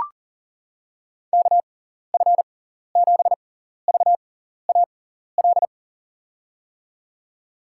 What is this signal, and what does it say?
Morse code, heard as a keyed tone, about 40 wpm, spaced as for 14 wpm, 700 Hz. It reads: KF7VAL
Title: USA Call Signs - 6 Characters - 6x Character Spacing - 40wpm